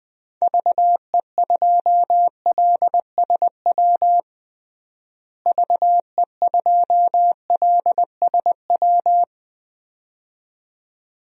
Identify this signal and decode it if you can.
Morse code, heard as a keyed tone, about 20 wpm, 700 Hz. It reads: VE2LSW VE2LSW
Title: DXCC-100 Call Signs - 6 Characters - Repeated 2x - 20wpm